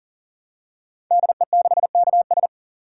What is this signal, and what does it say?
Morse code, heard as a keyed tone, about 42 wpm, 700 Hz. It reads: DE6KS
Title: Code Groups - 5 Letter:Numbers - 40wpm